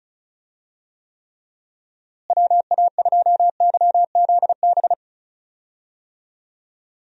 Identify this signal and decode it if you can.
Morse code, heard as a keyed tone, about 36 wpm, 700 Hz. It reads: WA2YZB
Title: USA Call Signs - 6 Characters - Repeated 3x - 35wpm